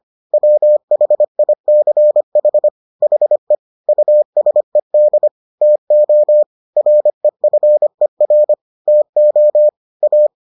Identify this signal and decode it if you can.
Morse code, heard as a keyed tone, about 25 wpm, 600 Hz. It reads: WHICH HE USED TO REFER TO A